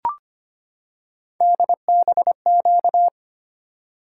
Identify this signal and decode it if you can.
Morse code, heard as a keyed tone, about 25 wpm, 700 Hz. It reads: DBQ
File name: Three Letter-Number Combinations - 25wpm